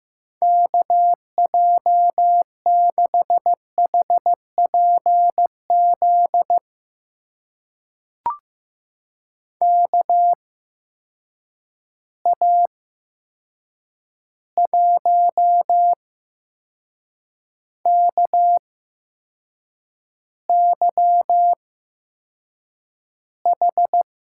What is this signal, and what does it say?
Morse code, heard as a keyed tone, about 15 wpm, spaced as for 6 wpm, 700 Hz. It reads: KJ6HPZ KA1KYH
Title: USA Call Signs - 6 Characters - 8x Character Spacing - 15wpm